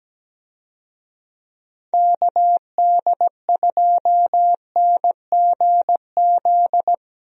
Morse code, heard as a keyed tone, about 17 wpm, 700 Hz. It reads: KD2NGZ
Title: USA Call Signs - 6 Characters - Repeated 2x - 17wpm